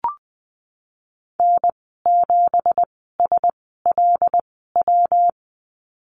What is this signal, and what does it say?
Morse code, heard as a keyed tone, about 20 wpm, spaced as for 15 wpm, 700 Hz. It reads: N7SLW
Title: 167 Lesson 27g - Number 7 - Call Signs 20wpm